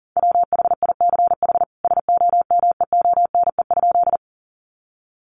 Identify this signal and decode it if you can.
Morse code, heard as a keyed tone, about 40 wpm, 700 Hz. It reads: WHICH SOMEONE?
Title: Sentences from Top 600 Words 40wpm